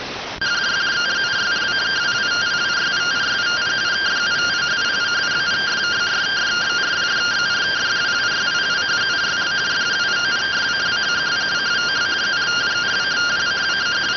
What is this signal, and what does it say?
Two-tone teleprinter signal, about 170 Hz shift.